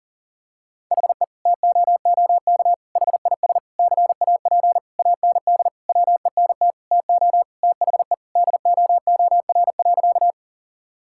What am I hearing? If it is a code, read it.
Morse code, heard as a keyed tone, about 40 wpm, 700 Hz. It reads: HE TOOK HIS CAP AND WENT TO THE DOOR.
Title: Sentences from Top 1600 Words 40wpm